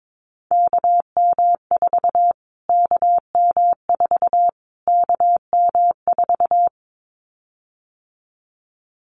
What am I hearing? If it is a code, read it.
Morse code, heard as a keyed tone, about 22 wpm, 700 Hz. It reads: KM4 KM4 KM4
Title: Callsign Prefixes for Top 25 DXCC - Repeated 3x - 22wpm